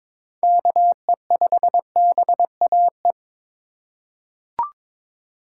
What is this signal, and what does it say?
Morse code, heard as a keyed tone, about 22 wpm, 700 Hz. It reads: KE5BAE
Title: USA Call Signs - 6 Characters - 4x Character Spacing - 22wpm